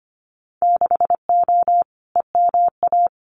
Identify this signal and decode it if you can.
Morse code, heard as a keyed tone, about 25 wpm, 700 Hz. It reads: 6O EMA
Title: Winter Field Day Exchanges Encore 25wpm